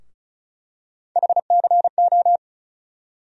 Morse code, heard as a keyed tone, about 36 wpm, 700 Hz. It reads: HCO
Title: USA Call Sign Postfixes - 3 Letters 35wpm